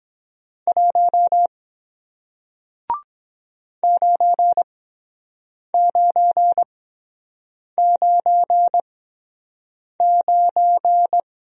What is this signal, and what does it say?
Morse code, heard as a keyed tone, about 23 wpm, spaced as for 7 wpm, 700 Hz. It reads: 1 9999